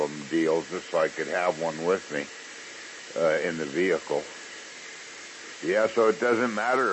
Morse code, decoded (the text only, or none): none